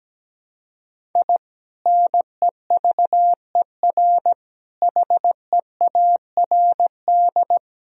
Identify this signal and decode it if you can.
Morse code, heard as a keyed tone, about 17 wpm, 700 Hz. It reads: I NEVER HEARD
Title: Sentences from Top 1000 Words - Review Part 3 - 17wpm